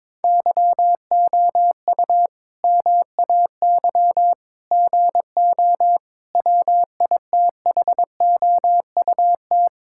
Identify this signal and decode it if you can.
Morse code, heard as a keyed tone, about 22 wpm, 700 Hz. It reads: YOU MAY GO WITHOUT
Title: Sentences from Top 1600 Words 22wpm